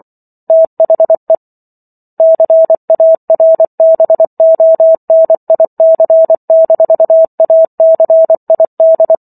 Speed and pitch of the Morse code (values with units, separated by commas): 24 wpm, 650 Hz